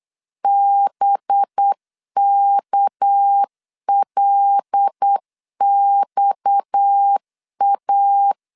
Morse code, heard as a keyed tone, about 8 wpm, 800 Hz. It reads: BKLXA
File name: LEZIONI / 6 PKX0 / 6 PKX 8wpm 3041